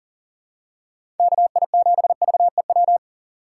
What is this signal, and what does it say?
Morse code, heard as a keyed tone, about 37 wpm, 700 Hz. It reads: KI7VEW
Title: USA Call Signs - 6x Character Spacing - 40wpm